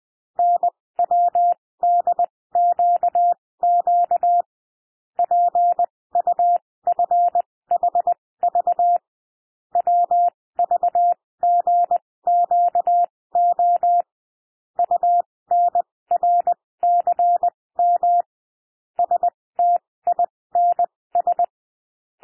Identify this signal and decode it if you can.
Morse code, heard as a keyed tone, about 20 wpm, spaced as for 16 wpm, 700 Hz. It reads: NWDQQ PUFHV WVGQO UNRCM STINS
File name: cwmp3 / L1 SOUND LETTERS 20 wpm